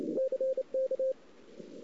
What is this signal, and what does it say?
Morse code, heard as a keyed tone, about 28 wpm, 530 Hz. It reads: CK
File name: CW / 2008-ss-cw-k5tr-op-n5rz -01:00